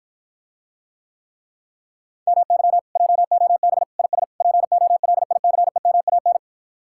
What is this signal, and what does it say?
Morse code, heard as a keyed tone, about 53 wpm, 700 Hz. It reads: MY JOB IS POLICEMAN